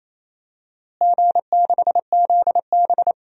Morse code, heard as a keyed tone, about 28 wpm, 700 Hz. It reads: G6ZB
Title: DXCC-100 Call Signs - 4 Characters - Repeated 3x - 28wpm